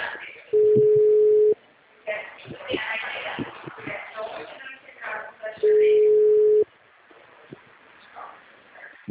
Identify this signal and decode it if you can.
Ringback tone (Europe, 425 Hz), with speech.
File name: out- -2021- - - .35